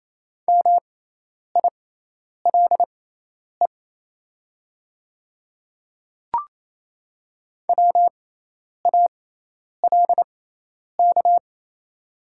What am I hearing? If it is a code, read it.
Morse code, heard as a keyed tone, about 27 wpm, spaced as for 10 wpm, 700 Hz. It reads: MILE WALK
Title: 6x Character Spacing - Top 300 Words - 28wpm